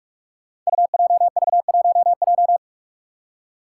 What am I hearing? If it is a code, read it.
Morse code, heard as a keyed tone, about 45 wpm, 700 Hz. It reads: UJV1J
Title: Five Challenging Letter-Number Combinations 45wpm